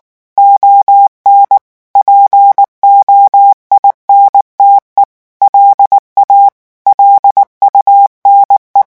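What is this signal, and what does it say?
Morse code, heard as a keyed tone, about 19 wpm, 800 Hz. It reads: ON POINTE LA LUNE
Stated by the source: French Morse revision 8 drill of random five-letter groups at 18 words per minute.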